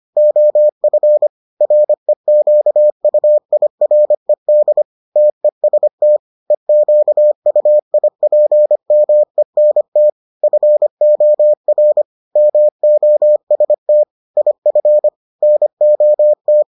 Morse code, heard as a keyed tone, about 25 wpm, 600 Hz. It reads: OF REQUIRED TEST EQUIPMENT FOR MOST IF NOT